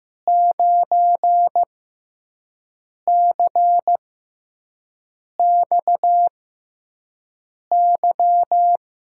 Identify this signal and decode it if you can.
Morse code, heard as a keyed tone, about 15 wpm, spaced as for 5 wpm, 700 Hz. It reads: 9CXY
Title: USA Call Signs - 6 Characters - 6x Character Spacing - 15wpm